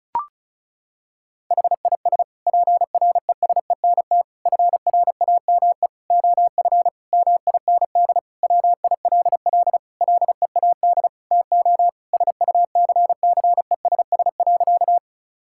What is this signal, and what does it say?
Morse code, heard as a keyed tone, about 34 wpm, 700 Hz. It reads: HIS PRESENT FRAME OF MIND WILL LEAD TO SUCCESS.
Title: Sentences from Top 1000 Words - Review Part 6 - 35wpm